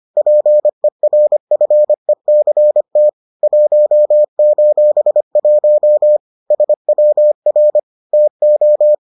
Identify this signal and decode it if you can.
Morse code, heard as a keyed tone, about 25 wpm, 600 Hz. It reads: PERFECT 1:1 SWR TO